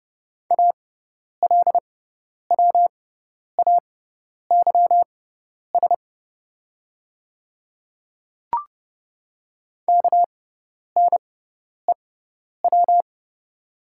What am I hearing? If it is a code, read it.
Morse code, heard as a keyed tone, about 30 wpm, spaced as for 10 wpm, 700 Hz. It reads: ALWAYS KNEW
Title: 6x Character Spacing - Top 500 Words - 30wpm